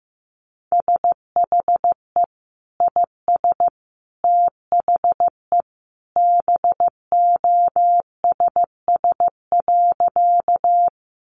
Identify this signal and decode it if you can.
Morse code, heard as a keyed tone, about 15 wpm, 700 Hz. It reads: SHE IS THE BOSS.
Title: Sentences from Top 700 Words 15wpm